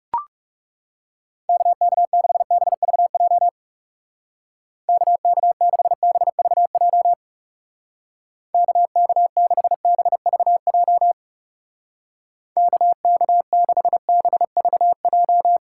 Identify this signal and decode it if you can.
Morse code, heard as a keyed tone, about 37 wpm, 700 Hz. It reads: KK6BVJ KK6BVJ KK6BVJ KK6BVJ
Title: Speed Racing - USA 2x3 Call Signs 30wpm